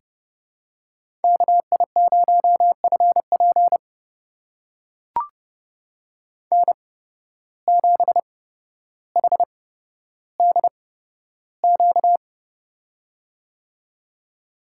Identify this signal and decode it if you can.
Morse code, heard as a keyed tone, about 30 wpm, spaced as for 13 wpm, 700 Hz. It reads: KI0FP N7HDQ
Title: USA Call Signs - 8x Character Spacing - 30wpm